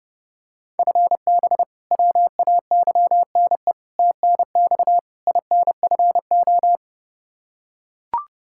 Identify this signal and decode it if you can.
Morse code, heard as a keyed tone, about 32 wpm, 700 Hz. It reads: FB WAYNE TNX INFO
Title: Practice QSO - Exchange 3 - More Info - 30wpm